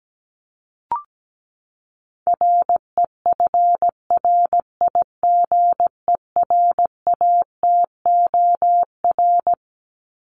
Morse code, heard as a keyed tone, about 17 wpm, 700 Hz. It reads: REFRIGERATOR
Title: Top 1501-1750 Words 17wpm